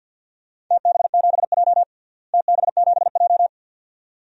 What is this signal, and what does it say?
Morse code, heard as a keyed tone, about 51 wpm, 700 Hz. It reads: TB7J TB7J
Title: DXCC-100 Call Signs - 4 Characters - Repeated 2x - 50wpm